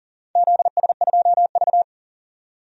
Morse code, heard as a keyed tone, about 40 wpm, 690 Hz. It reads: ZS2V
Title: DXCC-100 Call Signs - 4 Characters - 40wpm